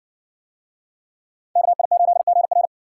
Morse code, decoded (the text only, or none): KI8GR